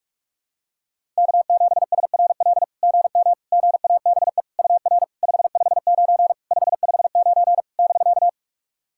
Morse code, heard as a keyed tone, about 44 wpm, 690 Hz. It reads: K7SRL GM GABE UR 559 559 <BK>
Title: SOTA QSO Practice 45wpm